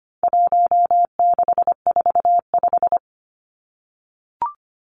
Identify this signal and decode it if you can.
Morse code, heard as a keyed tone, about 25 wpm, 700 Hz. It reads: 1645
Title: Numbers - 4-Digit 25wpm